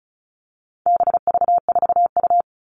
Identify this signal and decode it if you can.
Morse code, heard as a keyed tone, about 34 wpm, 700 Hz. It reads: BV4U
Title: All Cut Numbers - 4-Digit 35wpm